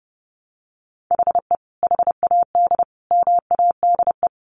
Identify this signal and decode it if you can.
Morse code, heard as a keyed tone, about 30 wpm, 700 Hz. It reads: HE HAD MADE
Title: Sentences from Top 900 Words 30wpm